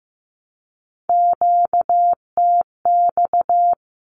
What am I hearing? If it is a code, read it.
Morse code, heard as a keyed tone, about 15 wpm, 700 Hz. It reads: QTX
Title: Q-Codes 15wpm